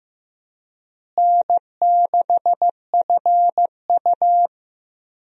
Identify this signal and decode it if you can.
Morse code, heard as a keyed tone, about 15 wpm, 700 Hz. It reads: N6FU
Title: USA Call Signs - 4 Characters - Repeated 3x - 15wpm